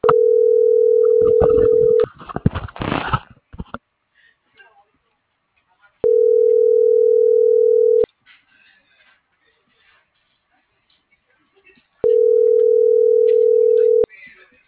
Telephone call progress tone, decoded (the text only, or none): ringback tone